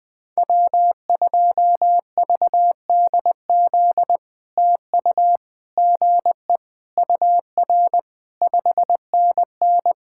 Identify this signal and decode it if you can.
Morse code, heard as a keyed tone, about 20 wpm, 700 Hz. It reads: W2VDZ TU GE UR 5NN
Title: POTA QSO Practice - 20wpm